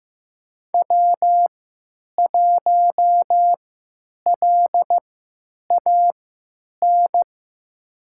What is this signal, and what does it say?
Morse code, heard as a keyed tone, about 15 wpm, spaced as for 8 wpm, 700 Hz. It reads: W1LAN
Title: USA Call Signs - 3x Character Spacing - 15wpm